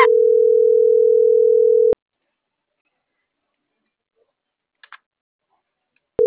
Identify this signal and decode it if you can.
Ringback tone (North America), 440 + 480 Hz.